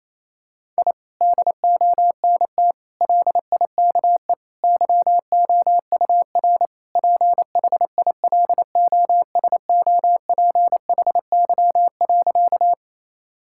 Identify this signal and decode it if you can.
Morse code, heard as a keyed tone, about 28 wpm, 700 Hz. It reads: I DONT LIKE YOUR PHILOSOPHY.
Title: Sentences from Top 1000 Words - Review Part 6 - 28wpm